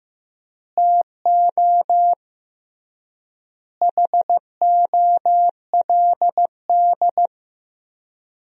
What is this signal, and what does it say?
Morse code, heard as a keyed tone, about 16 wpm, 700 Hz. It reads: TO HOLD
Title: 3x Word-Spacing - Sentences from Top 400 Words - 15wpm